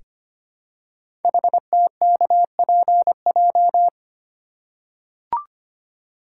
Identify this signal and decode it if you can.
Morse code, heard as a keyed tone, about 25 wpm, 700 Hz. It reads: HTKPJ
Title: Code Groups - 5 Letter:Numbers - 25wpm